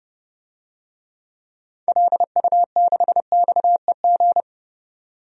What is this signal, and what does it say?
Morse code, heard as a keyed tone, about 29 wpm, 700 Hz. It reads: LU6XEG
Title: DXCC-100 Call Signs - 6 Characters - 30wpm